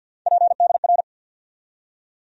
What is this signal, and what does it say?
Morse code, heard as a keyed tone, about 52 wpm, 700 Hz. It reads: PDR
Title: USA Call Sign Postfixes - 3 Letters 50wpm